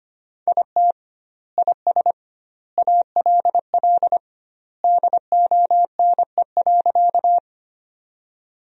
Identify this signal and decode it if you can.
Morse code, heard as a keyed tone, about 25 wpm, 700 Hz. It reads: IT IS ALL DONE.
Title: 2x Word-Spacing - Sentences from Top 500 Words - 25wpm